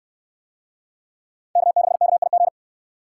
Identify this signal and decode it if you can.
Morse code, heard as a keyed tone, about 65 wpm, 700 Hz. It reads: K6KIG